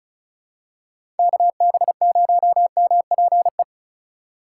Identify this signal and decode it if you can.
Morse code, heard as a keyed tone, about 36 wpm, 700 Hz. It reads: KB0MPE
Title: USA Call Signs - 6 Characters - 4x Character Spacing - 35wpm